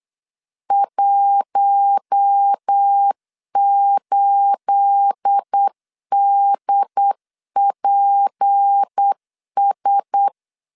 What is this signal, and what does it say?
Morse code, heard as a keyed tone, about 8 wpm, 800 Hz. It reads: 18DPS
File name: LEZIONI / 9 28 8wpm 3512